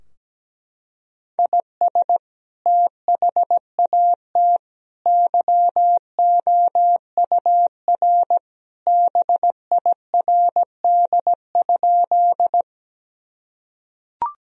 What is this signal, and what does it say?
Morse code, heard as a keyed tone, about 17 wpm, 700 Hz. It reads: IS THAT YOUR BIRD?
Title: Sentences from Top 1000 Words - Review Part 2 - 17wpm